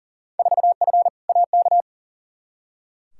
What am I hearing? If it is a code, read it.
Morse code, heard as a keyed tone, about 40 wpm, 700 Hz. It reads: 4F AK
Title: Field Day Exchanges Encore - 40wpm